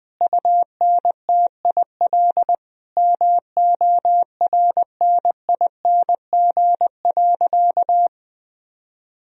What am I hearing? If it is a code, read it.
Morse code, heard as a keyed tone, about 21 wpm, 700 Hz. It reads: UNTIL MORNING.
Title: Sentences from Top 1000 Words - Review Part 1 - 20wpm